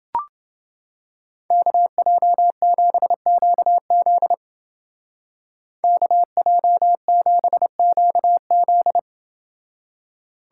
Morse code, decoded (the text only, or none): KJ7QZ KJ7QZ